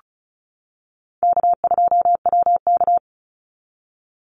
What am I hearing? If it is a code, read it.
Morse code, heard as a keyed tone, about 34 wpm, 700 Hz. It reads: K2WK
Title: Call Signs - 35wpm